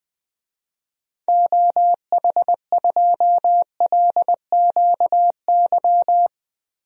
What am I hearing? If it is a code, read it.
Morse code, heard as a keyed tone, about 21 wpm, 700 Hz. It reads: OH2LQY
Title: DXCC-100 Call Signs - 6 Characters - 20wpm